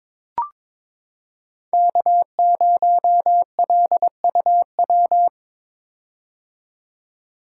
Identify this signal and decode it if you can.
Morse code, heard as a keyed tone, about 22 wpm, 700 Hz. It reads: K0LUW
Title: K1USN SST QSO Practice - 22wpm